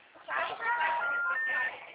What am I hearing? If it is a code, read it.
Special information tone, with speech.